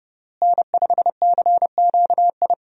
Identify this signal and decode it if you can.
Morse code, heard as a keyed tone, about 30 wpm, 700 Hz. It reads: N5CQI
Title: USA Call Signs - 5 Characters - Repeated 3x - 30wpm